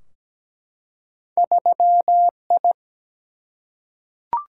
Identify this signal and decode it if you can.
Morse code, heard as a keyed tone, about 17 wpm, 700 Hz. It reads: 3I
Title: Single-Letter Single-Number 17wpm